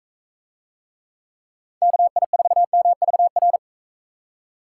Morse code, heard as a keyed tone, about 43 wpm, 700 Hz. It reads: KI4MVR